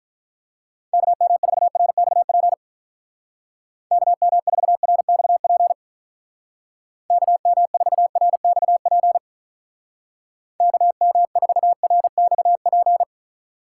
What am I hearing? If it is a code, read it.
Morse code, heard as a keyed tone, about 43 wpm, 700 Hz. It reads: KM4RXP KM4RXP KM4RXP KM4RXP
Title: Speed Racing - USA 2x3 Call Signs 35wpm